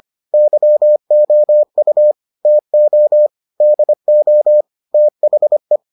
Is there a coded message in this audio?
Yes: Morse code from 0.3 to 5.8 s, heard as a keyed tone, about 25 wpm, 600 Hz. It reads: YOU TO DO THE